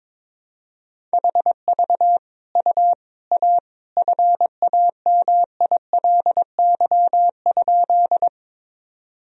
Morse code, heard as a keyed tone, about 22 wpm, 700 Hz. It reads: HV U A FAMILY?
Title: Sentences from Top 400 Words and Abbreviations 22wpm